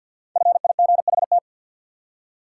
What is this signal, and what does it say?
Morse code, heard as a keyed tone, about 53 wpm, 700 Hz. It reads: FIGHT
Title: Five Letter Words - 50wpm